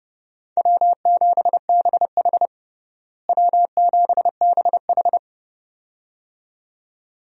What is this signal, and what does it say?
Morse code, heard as a keyed tone, about 30 wpm, 700 Hz. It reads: W7BH W7BH
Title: USA Call Signs - 4 Characters - Repeated 2x - 30wpm